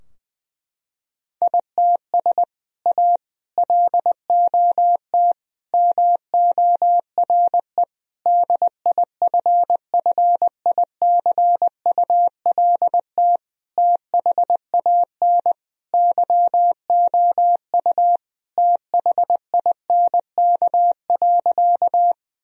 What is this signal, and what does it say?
Morse code, heard as a keyed tone, about 21 wpm, 700 Hz. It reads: ITS A LOT MORE DIFFICULT THAN YOU THINK.
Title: Sentences from Top 1000 Words - Review Part 3 - 20wpm